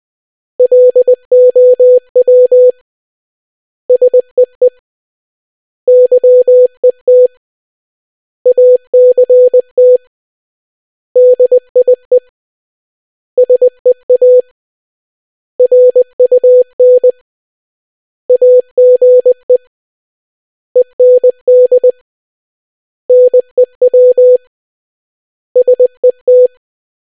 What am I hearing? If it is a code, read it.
Morse code, heard as a keyed tone, about 20 wpm, 500 Hz. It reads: LOW SEE YET ACT DIE SEA RUN AGE END NEW SET